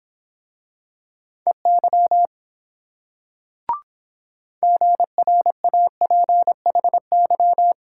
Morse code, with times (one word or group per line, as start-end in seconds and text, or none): 1.5-2.3 EY
4.6-7.7 GRAPHY